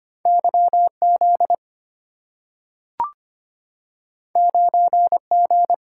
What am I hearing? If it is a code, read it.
Morse code, heard as a keyed tone, about 25 wpm, 690 Hz. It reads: YZ 9G
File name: Two Letter-Number Combinations - 25wpm